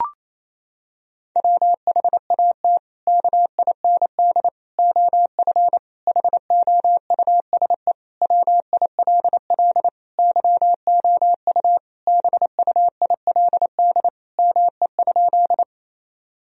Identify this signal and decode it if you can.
Morse code, heard as a keyed tone, about 28 wpm, 700 Hz. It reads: WHAT KIND OF HOUSE WILL YOU BUILD ME?
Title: Sentences from Top 1000 Words - Review Part 1 - 28wpm